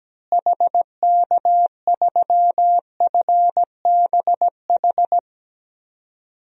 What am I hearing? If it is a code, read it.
Morse code, heard as a keyed tone, about 17 wpm, 700 Hz. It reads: HK3FBH
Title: DXCC-100 Call Signs - 6 Characters - Repeated 2x - 17wpm